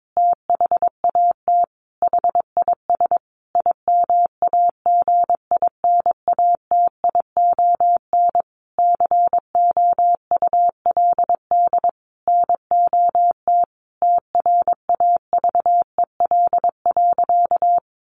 Morse code, heard as a keyed tone, about 22 wpm, 700 Hz. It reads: THAT HIS IMAGINATION COULD NOT TRAVEL.
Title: Sentences from Top 1300 Words 22wpm